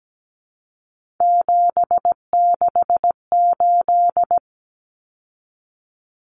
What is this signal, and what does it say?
Morse code, heard as a keyed tone, about 17 wpm, 700 Hz. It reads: 768
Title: Common Cut Numbers - 3-Digit 17wpm